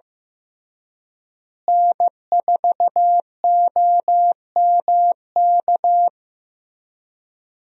Morse code, heard as a keyed tone, about 16 wpm, 700 Hz. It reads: N4OMK
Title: USA Call Signs - 5 Characters - Repeated 3x - 15wpm